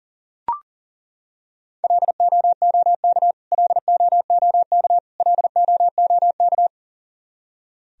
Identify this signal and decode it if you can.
Morse code, heard as a keyed tone, about 40 wpm, 700 Hz. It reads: LOOK LOOK LOOK
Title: Top 100 Words Repeated 40wpm